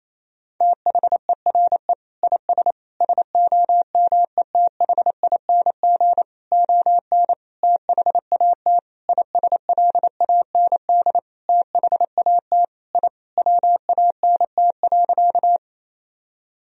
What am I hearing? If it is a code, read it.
Morse code, heard as a keyed tone, about 28 wpm, 700 Hz. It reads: THERE IS SOMETHING ON THAT ISLAND THAT I WANT.
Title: Sentences from Top 1000 Words - Review Part 3 - 28wpm